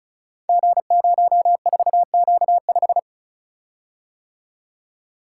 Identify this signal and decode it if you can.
Morse code, heard as a keyed tone, about 34 wpm, 700 Hz. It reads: G04Q5
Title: Code Groups - 5 Letter:Numbers - 35wpm